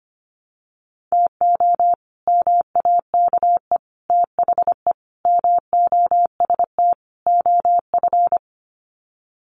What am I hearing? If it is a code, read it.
Morse code, heard as a keyed tone, about 25 wpm, 700 Hz. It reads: TO MAKE THE MOST OF
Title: Sets of 5 Words Encore2 25wpm